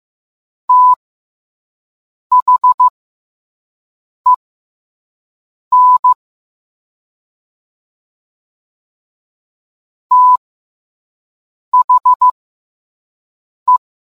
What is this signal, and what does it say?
Morse code, heard as a keyed tone, about 15 wpm, spaced as for 5 wpm, 1000 Hz. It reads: THEN THE